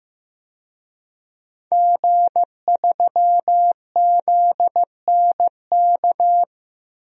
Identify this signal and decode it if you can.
Morse code, heard as a keyed tone, about 15 wpm, 700 Hz. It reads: G3ZNK
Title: DXCC-100 Call Signs - 5 Characters - Repeated 3x - 15wpm